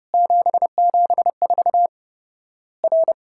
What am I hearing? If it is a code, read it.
Morse code, heard as a keyed tone, about 30 wpm, 700 Hz. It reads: 774 R
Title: Worked All Europe DX Contest - QTC Exchange - 30wpm